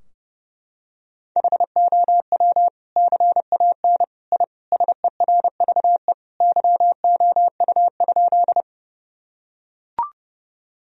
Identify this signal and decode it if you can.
Morse code, heard as a keyed tone, about 30 wpm, 700 Hz. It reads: HOW CAN I SERVE YOU?
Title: Sentences from Top 1000 Words - Review Part 2 - 30wpm